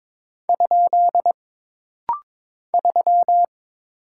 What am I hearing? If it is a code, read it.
Morse code, heard as a keyed tone, about 22 wpm, 700 Hz. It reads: ? 3
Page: This rapid-fire timing is geared to push your ICR (Instant Character Recognition) skills.